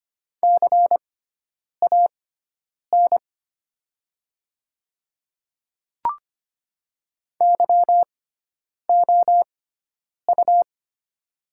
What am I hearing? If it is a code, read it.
Morse code, heard as a keyed tone, about 25 wpm, spaced as for 9 wpm, 700 Hz. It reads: CAN YOU
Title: 6x Character Spacing - Top 100 Words - 25wpm